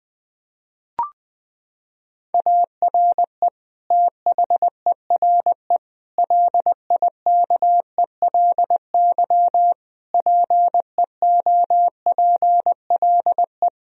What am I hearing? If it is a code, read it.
Morse code, heard as a keyed tone, about 19 wpm, 700 Hz. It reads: ARE THERE LIKELY PEOPLE
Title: Sentences from Top 1000 Words - Review Part 4 - 20wpm